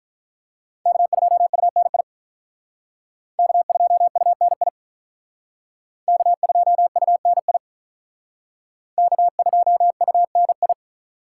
Morse code, heard as a keyed tone, about 43 wpm, 700 Hz. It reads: K2UNI K2UNI K2UNI K2UNI
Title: Speed Racing - USA 1x3 Call Signs 35wpm